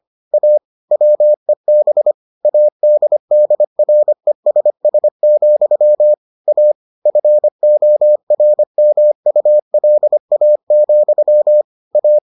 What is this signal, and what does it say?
Morse code, heard as a keyed tone, about 25 wpm, 600 Hz. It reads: A WEB ADDRESS, A FORMULA, A